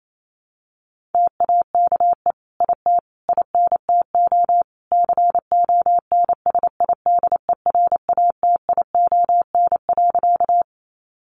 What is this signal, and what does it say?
Morse code, heard as a keyed tone, about 28 wpm, 700 Hz. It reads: TAKE IT INTO CONSIDERATION.
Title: Sentences from Top 1500 Words 28wpm